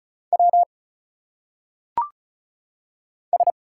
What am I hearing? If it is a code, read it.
Morse code, heard as a keyed tone, about 36 wpm, 700 Hz. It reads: W S
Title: Single Letters - Phonetic Pronunciation 35wpm